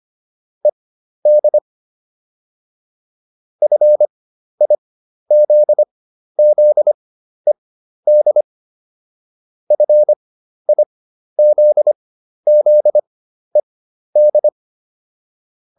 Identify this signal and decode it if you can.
Morse code, heard as a keyed tone, about 25 wpm, spaced as for 12 wpm, 600 Hz. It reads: ED FIZZED FIZZED